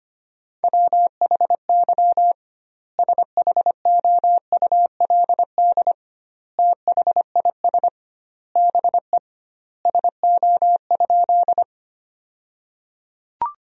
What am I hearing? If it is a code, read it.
Morse code, heard as a keyed tone, about 25 wpm, 700 Hz. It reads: WHY SHOULD THIS BE SO?
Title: 2x Word-Spacing - Sentences from Top 300 Words - 25wpm